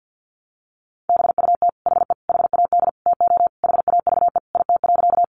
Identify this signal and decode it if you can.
Morse code, heard as a keyed tone, about 50 wpm, 700 Hz. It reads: BUT HE HAD TO HAVE IT.
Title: Sentences from Top 100 Words 50wpm